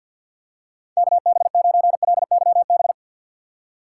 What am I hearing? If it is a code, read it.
Morse code, heard as a keyed tone, about 50 wpm, 700 Hz. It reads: KB9LYB
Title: USA Call Signs - 6 Characters - 3x Character Spacing - 50wpm